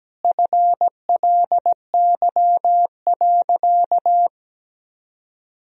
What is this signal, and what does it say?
Morse code, heard as a keyed tone, about 17 wpm, 700 Hz. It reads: FLY.
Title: Sentences from Top 1000 Words - Review Part 2 - 17wpm